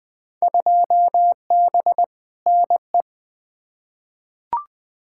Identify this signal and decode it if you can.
Morse code, heard as a keyed tone, about 21 wpm, 700 Hz. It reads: 2B NE
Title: Field Day Exchanges Encore - 20wpm